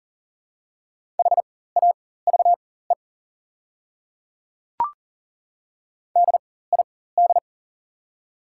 Morse code, heard as a keyed tone, about 40 wpm, spaced as for 19 wpm, 690 Hz. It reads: HAVE DID
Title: 4x Character Spacing - Top 100 Words - 40wpm